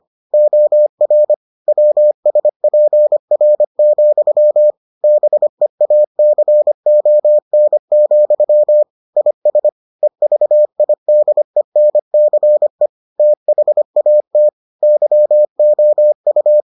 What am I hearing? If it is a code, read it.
Morse code, heard as a keyed tone, about 25 wpm, 600 Hz. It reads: OR WSPR, BEACON, IS EVIDENCE THAT YOU